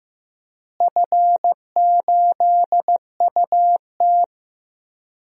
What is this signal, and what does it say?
Morse code, heard as a keyed tone, about 15 wpm, 700 Hz. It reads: F8UT
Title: DXCC-100 Call Signs - 4 Characters - Repeated 3x - 15wpm